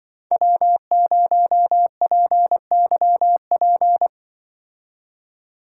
Morse code, heard as a keyed tone, about 23 wpm, 700 Hz. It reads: W0PYP
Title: Speed Racing - USA 1x3 Call Signs 20wpm